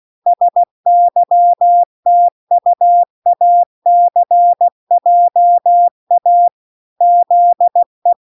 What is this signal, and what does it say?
Morse code, heard as a keyed tone, about 16 wpm, 700 Hz. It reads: SYTUACJA ZE